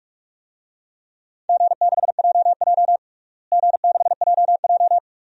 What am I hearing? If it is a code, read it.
Morse code, heard as a keyed tone, about 41 wpm, 700 Hz. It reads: G6JJ G6JJ
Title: DXCC-100 Call Signs - 4 Characters - Repeated 2x - 45wpm